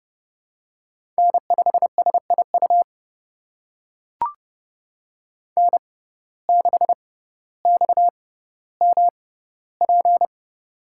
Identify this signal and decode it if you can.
Morse code, heard as a keyed tone, about 30 wpm, spaced as for 15 wpm, 700 Hz. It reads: N5SIU N6XMP
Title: USA Call Signs - 5 Characters - 6x Character Spacing - 30wpm